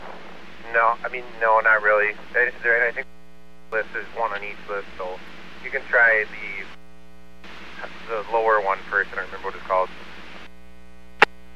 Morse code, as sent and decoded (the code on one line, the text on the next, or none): none